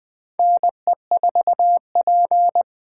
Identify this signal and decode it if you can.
Morse code, heard as a keyed tone, about 20 wpm, 700 Hz. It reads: NE4P
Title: USA Call Signs - 4 Characters - 3x Character Spacing - 20wpm